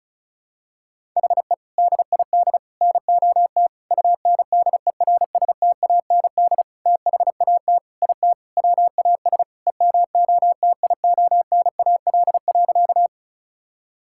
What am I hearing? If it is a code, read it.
Morse code, heard as a keyed tone, about 34 wpm, 700 Hz. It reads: HE DID NOT UNDERSTAND THAT IT WAS EMOTIONAL.
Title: Sentences from Top 1000 Words - Review Part 5 - 35wpm